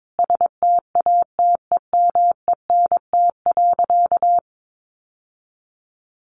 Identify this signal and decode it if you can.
Morse code, heard as a keyed tone, about 22 wpm, 700 Hz. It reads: STATEMENT.
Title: Sentences from Top 1200 Words 22wpm